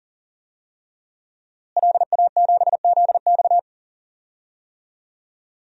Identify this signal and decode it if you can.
Morse code, heard as a keyed tone, about 40 wpm, 700 Hz. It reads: LA7ZX
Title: DXCC-100 Call Signs - 5 Characters - Repeated 2x - 40wpm